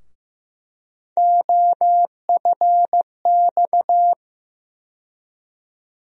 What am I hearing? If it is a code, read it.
Morse code, heard as a keyed tone, about 15 wpm, 700 Hz. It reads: OFX
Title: USA Call Sign Postfixes - 3 Letters 15wpm